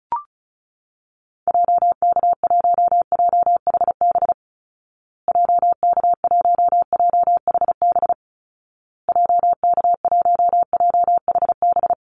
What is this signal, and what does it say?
Morse code, heard as a keyed tone, about 34 wpm, 700 Hz. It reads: JK1JHB JK1JHB JK1JHB
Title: Call Signs Repeated 2x - 35wpm